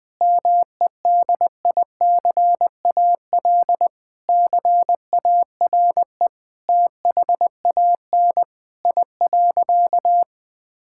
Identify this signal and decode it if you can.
Morse code, heard as a keyed tone, about 20 wpm, 700 Hz. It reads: MEDICAL CARE THAN I.
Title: Sentences from Top 1000 Words - Review Part 5 - 20wpm